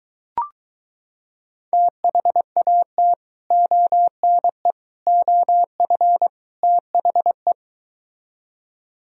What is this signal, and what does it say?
Morse code, heard as a keyed tone, about 23 wpm, 700 Hz. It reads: THAT ONE OF THE